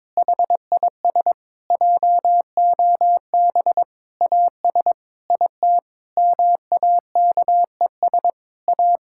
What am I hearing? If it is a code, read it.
Morse code, heard as a keyed tone, about 22 wpm, 700 Hz. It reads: HIS JOB AS IT MAKES A
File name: Difficult Sentences with Hints - 22wpm